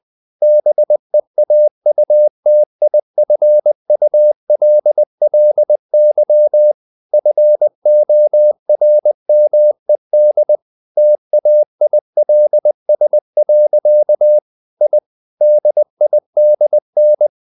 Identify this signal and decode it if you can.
Morse code, heard as a keyed tone, about 20 wpm, 600 Hz. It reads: BEAUTIFULLY FORMED TAILS. I DIDN
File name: the old man and the sea part 2 20wpm